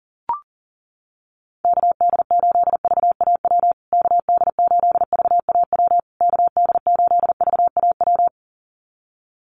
Morse code, heard as a keyed tone, about 40 wpm, 700 Hz. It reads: KD8VAW KD8VAW KD8VAW
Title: USA Call Signs Repeated 3x 40wpm